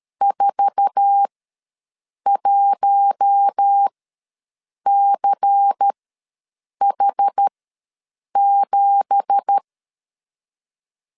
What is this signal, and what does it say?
Morse code, heard as a keyed tone, about 13 wpm, spaced as for 7 wpm, 800 Hz. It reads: 41CH7
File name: LEZIONI / 10 SIGNS / 10B INT ws 12wpm 3527